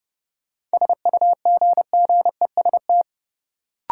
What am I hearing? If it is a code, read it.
Morse code, heard as a keyed tone, about 30 wpm, 700 Hz. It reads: SUGGEST
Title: Sentences from Top 1300 Words - New Word Drill - 30wpm